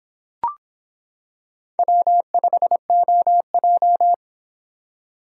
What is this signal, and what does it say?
Morse code, heard as a keyed tone, about 25 wpm, 700 Hz. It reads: W5OJ